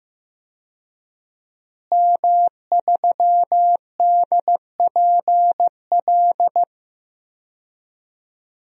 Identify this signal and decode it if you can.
Morse code, heard as a keyed tone, about 15 wpm, 700 Hz. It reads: M3DPL
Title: DXCC-100 Call Signs - 5 Characters - Repeated 3x - 15wpm